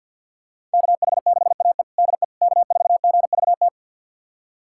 Morse code, heard as a keyed tone, about 52 wpm, 690 Hz. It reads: KH6AE DE K4GVT
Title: USA Callsign DE Callsign - 5 Characters - 50wpm